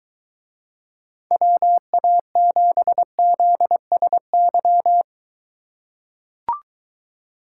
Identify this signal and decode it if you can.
Morse code, heard as a keyed tone, about 23 wpm, 700 Hz. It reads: WA7ZSY